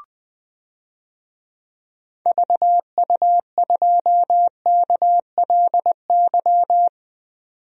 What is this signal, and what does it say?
Morse code, heard as a keyed tone, about 20 wpm, 700 Hz. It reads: VU2KLY
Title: DXCC-100 Call Signs - 6 Characters - 20wpm